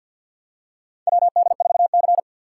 Morse code, heard as a keyed tone, about 53 wpm, 700 Hz. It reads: WD4C
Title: USA Call Signs - 4 Characters - 3x Character Spacing - 50wpm